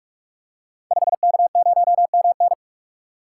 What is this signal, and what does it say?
Morse code, heard as a keyed tone, about 44 wpm, 700 Hz. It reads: 5K0MN
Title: Code Groups - 5 Letter:Numbers - 45wpm